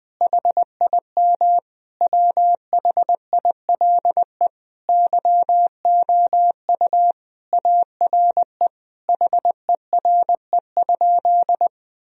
Morse code, heard as a keyed tone, about 21 wpm, 700 Hz. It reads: HIM WHILE YOU ARE HERE?
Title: Sentences from Top 1000 Words - Review Part 1 - 20wpm